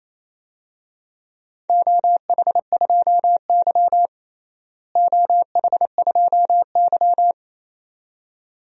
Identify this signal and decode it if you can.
Morse code, heard as a keyed tone, about 27 wpm, 700 Hz. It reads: OH2Y OH2Y
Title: DXCC-100 Call Signs - 4 Characters - Repeated 2x - 28wpm